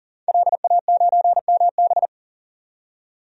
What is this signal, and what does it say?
Morse code, heard as a keyed tone, about 40 wpm, 700 Hz. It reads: LA9MB
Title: DXCC-100 Call Signs - 5 Characters - 40wpm